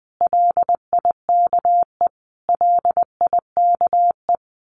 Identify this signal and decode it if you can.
Morse code, heard as a keyed tone, about 20 wpm, 700 Hz. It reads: LIKE LIKE
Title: Top 300 Words Repeated 20wpm